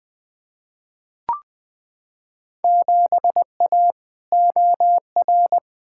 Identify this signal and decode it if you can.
Morse code, heard as a keyed tone, about 21 wpm, 700 Hz. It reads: 7A OR
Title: Field Day Exchanges Encore - 20wpm